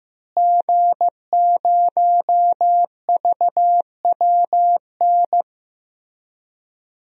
Morse code, heard as a keyed tone, about 15 wpm, 700 Hz. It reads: G0VWN
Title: DXCC-100 Call Signs - 5 Characters - Repeated 2x - 15wpm